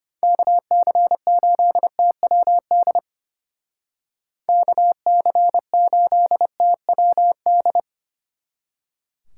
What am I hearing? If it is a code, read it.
Morse code, heard as a keyed tone, about 32 wpm, 700 Hz. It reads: KC8TWD KC8TWD